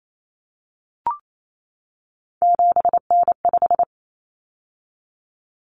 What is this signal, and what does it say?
Morse code, heard as a keyed tone, about 28 wpm, 700 Hz. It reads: 7N5
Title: Common Cut Numbers - 3-Digit 28wpm